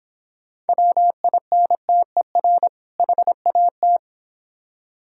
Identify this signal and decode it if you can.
Morse code, heard as a keyed tone, about 26 wpm, 700 Hz. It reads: WINTER HAT